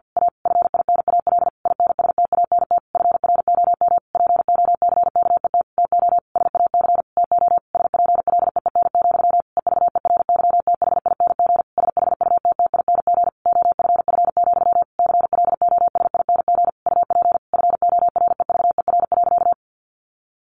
Morse code, heard as a keyed tone, about 48 wpm, 700 Hz. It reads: A FINAL INSTANT FROM POCKET TO SAC TO SPLEEN, EVERYTHING SHUTTING OFF, CLOSING UP FOREVER.
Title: A Sound Of Thunder 50wpm